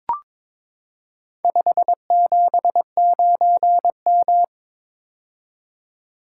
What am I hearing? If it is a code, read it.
Morse code, heard as a keyed tone, about 22 wpm, 700 Hz. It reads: 579M
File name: practice / 22wpm / Four Letter-Number Combinations - 22wpm